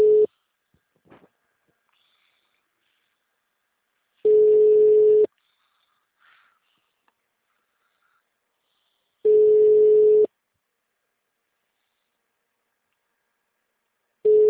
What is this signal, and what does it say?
Ringback tone (Europe, 425 Hz).
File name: out- -2007- - - .10320